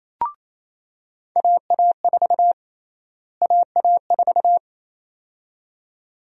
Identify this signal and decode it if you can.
Morse code, heard as a keyed tone, about 28 wpm, 700 Hz. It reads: AA4 AA4
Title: USA Call Sign Prefixes - Repeated 2x - 28wpm